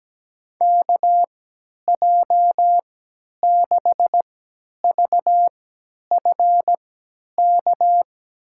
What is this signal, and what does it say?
Morse code, heard as a keyed tone, about 17 wpm, spaced as for 10 wpm, 700 Hz. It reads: KJ6VFK
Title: USA Call Signs - 6 Characters - 3x Character Spacing - 17wpm